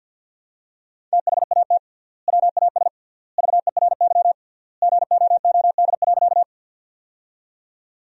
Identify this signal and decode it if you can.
Morse code, heard as a keyed tone, about 49 wpm, 700 Hz. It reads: THAT WAS VERY GOOD.
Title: 3x Word-Spacing - Sentences from Top 200 Words - 50wpm